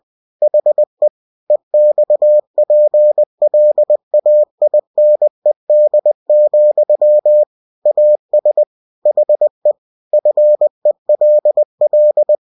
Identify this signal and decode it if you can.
Morse code, heard as a keyed tone, about 20 wpm, 600 Hz. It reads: HE EXPLAINED, AS HE FELL
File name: cw-books / winnie-the-pooh / winnie-20wpm-page-0054